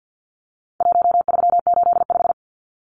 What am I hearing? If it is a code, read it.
Morse code, heard as a keyed tone, about 50 wpm, 700 Hz. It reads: 1385
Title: Numbers - 4-Digit 50wpm